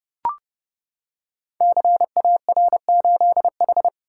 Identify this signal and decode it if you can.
Morse code, heard as a keyed tone, about 30 wpm, 700 Hz. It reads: CAR8H
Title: Code Groups - 5 Letter:Numbers - 30wpm